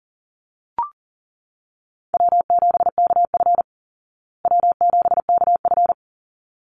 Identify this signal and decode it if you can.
Morse code, heard as a keyed tone, about 40 wpm, 700 Hz. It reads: W7KF W7KF
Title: Call Signs Repeated 3x - 40wpm